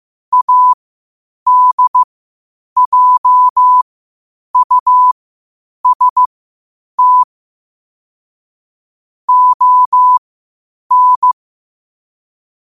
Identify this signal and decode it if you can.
Morse code, heard as a keyed tone, about 15 wpm, spaced as for 8 wpm, 1000 Hz. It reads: ADJUST ON
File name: qotdfiles / 8wpm